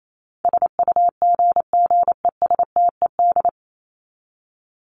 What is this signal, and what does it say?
Morse code, heard as a keyed tone, about 28 wpm, 700 Hz. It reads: SUGGESTED
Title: ED Suffix - Top 100 Words with Base Words 28wpm